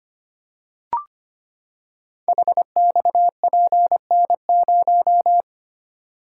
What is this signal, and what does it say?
Morse code, heard as a keyed tone, about 25 wpm, 700 Hz. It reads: HXPN0
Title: Code Groups - 5 Letter:Numbers - 25wpm